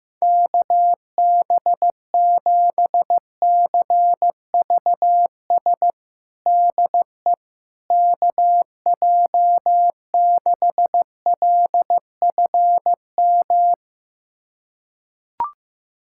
Morse code, heard as a keyed tone, about 15 wpm, 700 Hz. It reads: KB7CVS DE KJ6LFM
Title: USA Callsign DE Callsign - 6 Characters - 15wpm